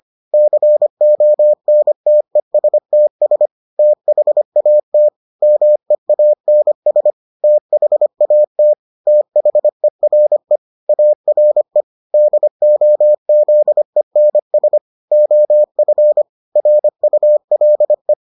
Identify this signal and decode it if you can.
Morse code, heard as a keyed tone, about 25 wpm, 600 Hz. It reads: CONTESTS THAT MEANS THAT THERE ARE DOZENS OF RULE